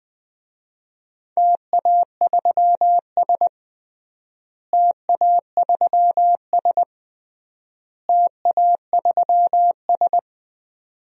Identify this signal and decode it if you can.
Morse code, heard as a keyed tone, about 20 wpm, 700 Hz. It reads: TA3S TA3S TA3S
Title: DXCC-100 Call Signs - 4 Characters - Repeated 3x - 20wpm